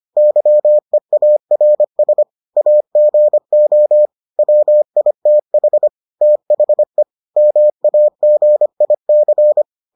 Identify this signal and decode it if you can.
Morse code, heard as a keyed tone, about 25 wpm, 600 Hz. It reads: YEARS AGO WITH THE MAGIC